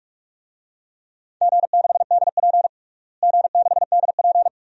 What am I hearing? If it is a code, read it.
Morse code, heard as a keyed tone, about 41 wpm, 700 Hz. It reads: G6DP G6DP
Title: DXCC-100 Call Signs - 4 Characters - Repeated 2x - 45wpm